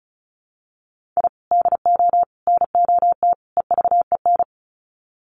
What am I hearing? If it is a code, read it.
Morse code, heard as a keyed tone, about 34 wpm, 700 Hz. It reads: I DO NOT EVEN
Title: Sets of 4 Words 35wpm